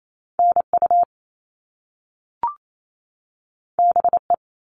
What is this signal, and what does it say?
Morse code, heard as a keyed tone, about 28 wpm, 700 Hz. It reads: NU BE
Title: All Cut Numbers - 2-Digit 28wpm